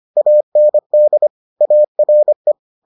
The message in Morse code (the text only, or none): AND ARE